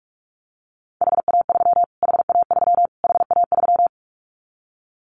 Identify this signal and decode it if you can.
Morse code, heard as a keyed tone, about 44 wpm, 700 Hz. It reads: HA3 HA3 HA3
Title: Callsign Prefixes for Top 25 DXCC - Repeated 3x - 45wpm